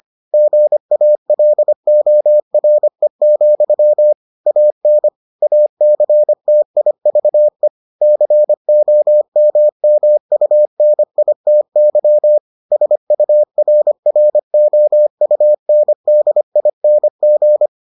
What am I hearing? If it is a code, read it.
Morse code, heard as a keyed tone, about 25 wpm, 600 Hz. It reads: GALORE, AN ACTIVE COMMUNITY SURROUNDING